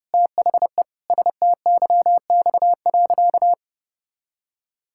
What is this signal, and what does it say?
Morse code, heard as a keyed tone, about 30 wpm, 700 Hz. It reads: THE STYX.
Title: Pangram Challenge v2 - 30wpm